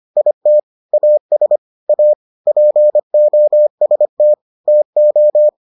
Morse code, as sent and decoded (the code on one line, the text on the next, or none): .. - / .- ... / .- / .--. --- ... - / - ---
IT AS A POST TO